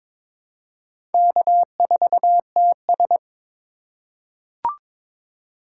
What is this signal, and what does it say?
Morse code, heard as a keyed tone, about 22 wpm, 700 Hz. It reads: K4TS
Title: USA Call Signs - 2x Character Spacing - 22wpm